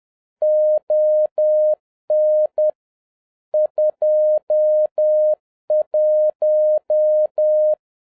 Morse code, heard as a keyed tone, about 10 wpm, 610 Hz. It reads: ON 21